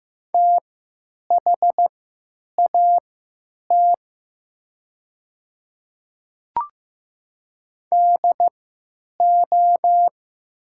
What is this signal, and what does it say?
Morse code, heard as a keyed tone, about 15 wpm, spaced as for 9 wpm, 700 Hz. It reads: THAT DO
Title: 3x Character Spacing - Top 100 Words - 15wpm